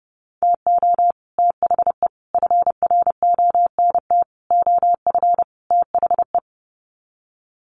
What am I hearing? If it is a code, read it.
Morse code, heard as a keyed tone, about 30 wpm, 700 Hz. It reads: TO THE FRONT OF THE
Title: Sets of 5 Words 30wpm